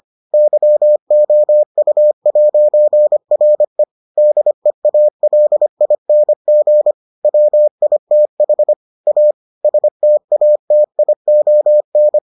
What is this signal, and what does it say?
Morse code, heard as a keyed tone, about 25 wpm, 600 Hz. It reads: YOU'RE DEALING WITH A STATION